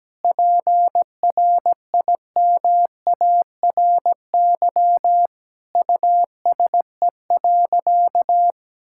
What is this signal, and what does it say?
Morse code, heard as a keyed tone, about 17 wpm, 700 Hz. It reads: PRIMARY USE.
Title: Sentences from Top 1000 Words - Review Part 5 - 17wpm